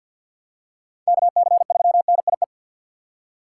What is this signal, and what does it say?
Morse code, heard as a keyed tone, about 48 wpm, 690 Hz. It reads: KC3NIE
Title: K1USN SST QSO Practice - 50wpm